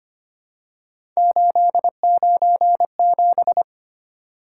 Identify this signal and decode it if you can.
Morse code, heard as a keyed tone, about 25 wpm, 700 Hz. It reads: 897
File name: Three Letter-Number Combinations - 25wpm